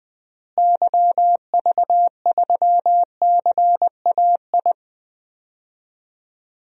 Morse code, heard as a keyed tone, about 20 wpm, 700 Hz. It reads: YV3CAI
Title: DXCC-100 Call Signs - 6 Characters - Repeated 3x - 20wpm